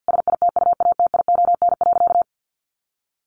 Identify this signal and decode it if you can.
Morse code, heard as a keyed tone, about 50 wpm, 700 Hz. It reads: SITUATION.
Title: Sentences from Top 1200 Words 50wpm